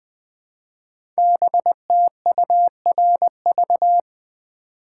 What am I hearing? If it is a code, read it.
Morse code, heard as a keyed tone, about 20 wpm, 700 Hz. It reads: BTURV
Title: Code Groups - 5 Letters - 20wpm